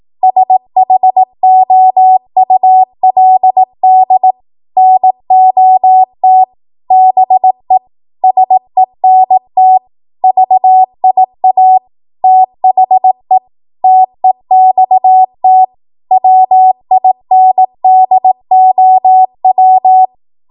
Morse code, heard as a keyed tone, about 18 wpm, 750 Hz. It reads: SHOULD NOT BE SENT VIA THE TEXT WINDOW
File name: Archive / 18 WPM / 18WPM